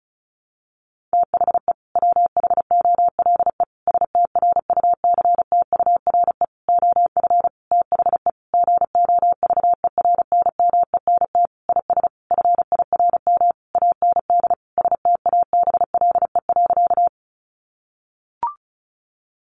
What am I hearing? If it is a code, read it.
Morse code, heard as a keyed tone, about 34 wpm, 700 Hz. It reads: THE WHOLE STRUCTURE OF THE GOVERNMENT IS FIRM AND STABLE.
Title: Sentences from Top 1500 Words 35wpm